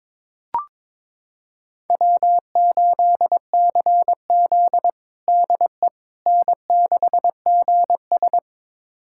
Morse code, heard as a keyed tone, about 22 wpm, 700 Hz. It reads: W8CZ DE N6GS
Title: USA Callsign DE Callsign - 4 Characters - 22wpm